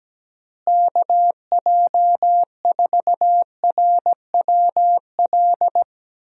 Morse code, heard as a keyed tone, about 17 wpm, 700 Hz. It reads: KJ4RWL
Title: USA Call Signs - 6 Characters - Repeated 3x - 17wpm